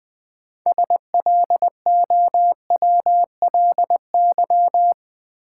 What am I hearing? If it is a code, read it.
Morse code, heard as a keyed tone, about 20 wpm, 700 Hz. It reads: SLOWLY
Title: Sentences from Top 1100 Words - New Word Drill - 20wpm